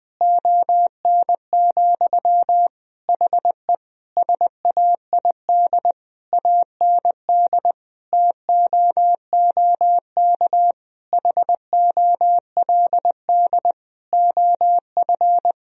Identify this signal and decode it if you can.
Morse code, heard as a keyed tone, about 21 wpm, 700 Hz. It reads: ON, HE SAID AND TOOK HOLD OF
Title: Sentences from Top 1000 Words - Review Part 2 - 20wpm